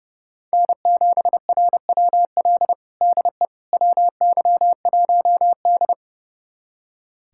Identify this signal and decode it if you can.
Morse code, heard as a keyed tone, about 30 wpm, 700 Hz. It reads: N7RWL DE WY1D
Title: USA Callsign DE Callsign 30wpm